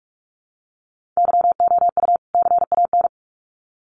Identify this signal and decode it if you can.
Morse code, heard as a keyed tone, about 44 wpm, 700 Hz. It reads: YOU CAN
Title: Sets of 2 Words 45wpm